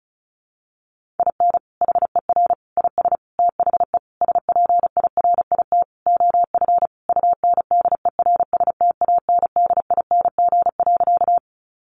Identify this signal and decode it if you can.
Morse code, heard as a keyed tone, about 34 wpm, 700 Hz. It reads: IN HER IS THE SPIRIT OF UNDERSTANDING.
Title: Sentences from Top 1400 Words 35wpm